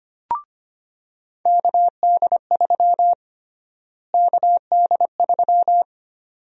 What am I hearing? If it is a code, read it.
Morse code, heard as a keyed tone, about 25 wpm, 700 Hz. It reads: KD3 KD3
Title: USA Call Sign Prefixes - Repeated 2x - 25wpm